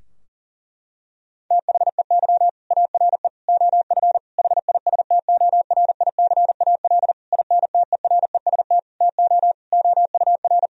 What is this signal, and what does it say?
Morse code, heard as a keyed tone, about 40 wpm, 700 Hz. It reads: THEY ARE OF HISTORICAL INTEREST TO OUR
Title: Sentences from Top 1000 Words - Review Part 4 - 40wpm